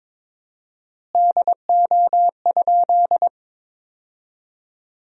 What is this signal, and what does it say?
Morse code, heard as a keyed tone, about 22 wpm, 700 Hz. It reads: DO?
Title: 4x Word-Spacing - Sentences from Top 100 Words - 22wpm